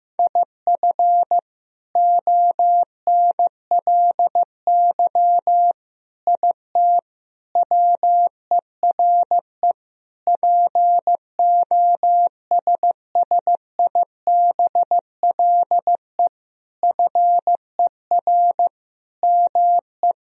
Morse code, heard as a keyed tone, about 15 wpm, 700 Hz. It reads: IF ONLY IT WERE POSSIBLE FER ME
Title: Sentences from Top 500 Words and Abbreviations 15wpm